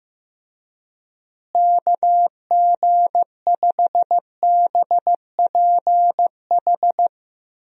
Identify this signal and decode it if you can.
Morse code, heard as a keyed tone, about 15 wpm, 700 Hz. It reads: KG5BPH
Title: USA Call Signs - 6 Characters - Repeated 3x - 15wpm